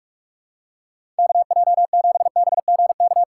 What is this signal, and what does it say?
Morse code, heard as a keyed tone, about 46 wpm, 700 Hz. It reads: KJ7BGK
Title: USA Call Signs - 6 Characters - 6x Character Spacing - 45wpm